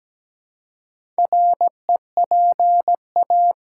Morse code, heard as a keyed tone, about 17 wpm, 700 Hz. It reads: REPA
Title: Six Letter Words - 17wpm